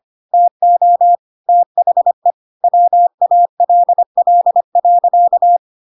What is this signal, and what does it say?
Morse code, heard as a keyed tone, about 25 wpm, 700 Hz. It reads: TO THE WALL.
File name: Pinocchio / Pinocchio ch3 25 25 ik1hge